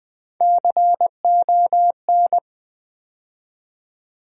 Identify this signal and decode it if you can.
Morse code, heard as a keyed tone, about 20 wpm, 700 Hz. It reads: CON
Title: Build A Word - 3 Letters At A Time - 9-Letter Words - 20wpm